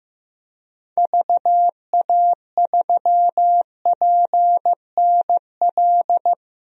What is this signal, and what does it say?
Morse code, heard as a keyed tone, about 15 wpm, 700 Hz. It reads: VA3PNL
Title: DXCC-100 Call Signs - 6 Characters - 15wpm